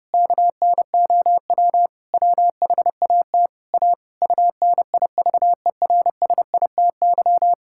Morse code, heard as a keyed tone, about 30 wpm, 700 Hz. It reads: KNOW WHAT A UNIVERSITY
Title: Sentences from Top 1000 Words - Review Part 5 - 30wpm